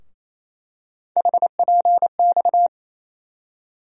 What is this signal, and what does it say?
Morse code, heard as a keyed tone, about 28 wpm, 700 Hz. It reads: HPX
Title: USA Call Sign Postfixes - 3 Letters 28wpm